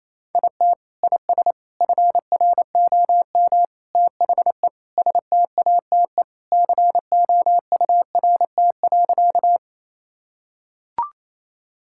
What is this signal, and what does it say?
Morse code, heard as a keyed tone, about 28 wpm, 700 Hz. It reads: IT IS FROM THE STATE COURT.
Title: Sentences from Top 1600 Words 28wpm